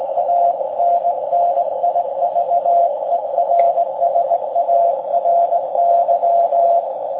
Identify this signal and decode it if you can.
Morse code, heard as a keyed tone, about 20 wpm, 690 Hz, with radio noise. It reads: ANNIVERSARY